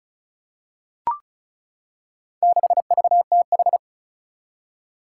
Morse code, heard as a keyed tone, about 34 wpm, 700 Hz. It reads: 6VTH
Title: Four Challenging Letter-Number Combinations 35wpm